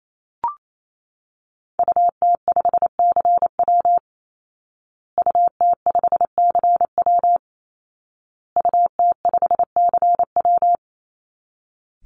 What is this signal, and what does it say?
Morse code, heard as a keyed tone, about 28 wpm, 700 Hz. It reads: UT5CW UT5CW UT5CW
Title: Call Signs Repeated 2x - 28wpm